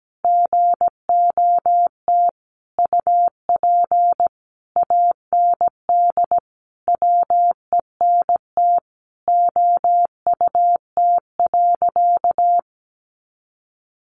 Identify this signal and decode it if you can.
Morse code, heard as a keyed tone, about 17 wpm, 700 Hz. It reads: GOT UP AND WENT OUT.
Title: Sentences from Top 300 Words 17wpm